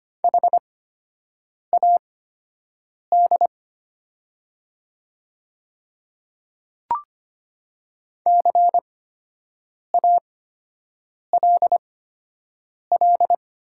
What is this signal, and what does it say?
Morse code, heard as a keyed tone, about 25 wpm, spaced as for 7 wpm, 700 Hz. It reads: HAD CALL
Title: 8x Character Spacing - Top 100 Words - 25wpm